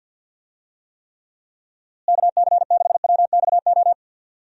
Morse code, heard as a keyed tone, about 51 wpm, 700 Hz. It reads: KC6WXO
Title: USA Call Signs - 6 Characters - Repeated 2x - 50wpm